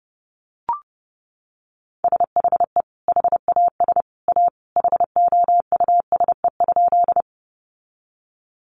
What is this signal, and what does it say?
Morse code, heard as a keyed tone, about 29 wpm, 700 Hz. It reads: SHE HAS A HOUSE?
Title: Sentences from Top 200 Words 30wpm